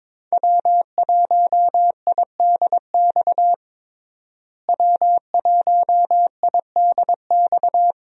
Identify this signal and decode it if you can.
Morse code, heard as a keyed tone, about 22 wpm, 700 Hz. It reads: W1IDX W1IDX
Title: USA Call Signs - 5 Characters - Repeated 2x - 22wpm